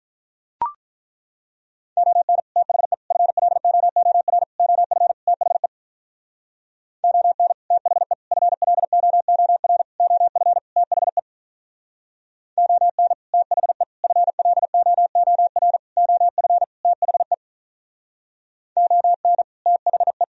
Morse code, heard as a keyed tone, about 43 wpm, 700 Hz. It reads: ON THE FLOOR OF THE ON THE FLOOR OF THE ON THE FLOOR OF THE ON THE
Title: Speed-Racing - Sets of 5 Words 35wpm